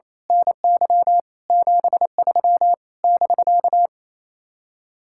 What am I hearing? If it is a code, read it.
Morse code, heard as a keyed tone, about 26 wpm, 700 Hz. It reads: NY 73 <BK>
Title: Speed-Racing Common QSO Phrases Encore - 28wpm